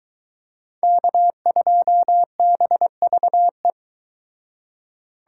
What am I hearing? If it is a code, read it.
Morse code, heard as a keyed tone, about 22 wpm, 700 Hz. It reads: K2BVE